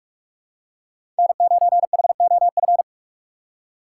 Morse code, heard as a keyed tone, about 46 wpm, 700 Hz. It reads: N9HOF
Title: USA Call Signs - 5 Characters - 3x Character Spacing - 45wpm